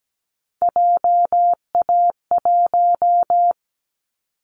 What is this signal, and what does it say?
Morse code, heard as a keyed tone, about 17 wpm, 700 Hz. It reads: JA1
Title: Callsign Prefixes for Top 25 DXCC - 17wpm